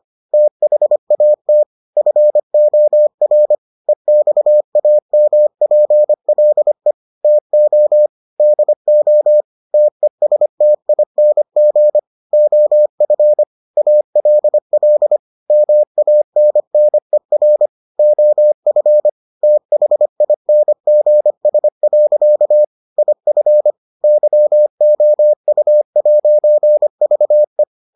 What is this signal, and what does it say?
Morse code, heard as a keyed tone, about 25 wpm, 600 Hz. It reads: THAT FOR EXAMPLE TO DO TESTING OF ALL MANNER OF THINGS. IF YOU'VE